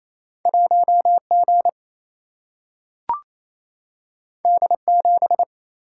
Morse code, heard as a keyed tone, about 28 wpm, 700 Hz. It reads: 1 G D 7
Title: Single-Letter Single-Number 28wpm